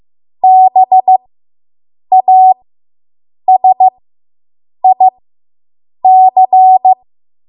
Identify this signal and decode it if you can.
Morse code, heard as a keyed tone, about 15 wpm, spaced as for 7 wpm, 750 Hz. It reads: BASIC